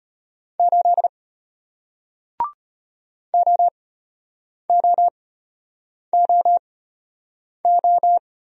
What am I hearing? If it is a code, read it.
Morse code, heard as a keyed tone, about 34 wpm, spaced as for 8 wpm, 700 Hz. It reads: 8 OOOO